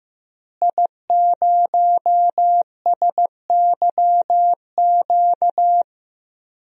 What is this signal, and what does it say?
Morse code, heard as a keyed tone, about 15 wpm, 700 Hz. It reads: I0SYQ
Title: DXCC-100 Call Signs - 5 Characters - Repeated 3x - 15wpm